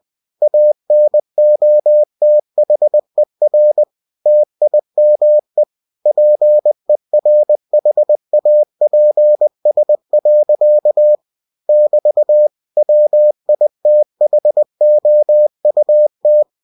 Morse code, heard as a keyed tone, about 20 wpm, 600 Hz. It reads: ANOTHER TIME PERHAPS. <BT> WITHOUT
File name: cw-books / winnie-the-pooh / winnie-20wpm-page-0063